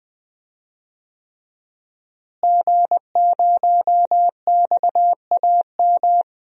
Morse code, heard as a keyed tone, about 20 wpm, 700 Hz. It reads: G0XAM
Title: DXCC-100 Call Signs - 5 Characters - Repeated 3x - 20wpm